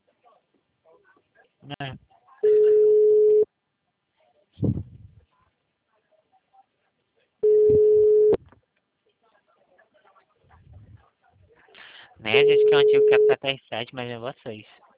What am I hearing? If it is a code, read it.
Ringback tone (Europe, 425 Hz), with speech.